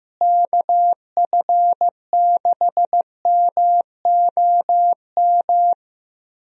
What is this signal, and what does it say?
Morse code, heard as a keyed tone, about 16 wpm, 700 Hz. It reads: KF6MOM
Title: USA Call Signs - 6 Characters - Repeated 3x - 15wpm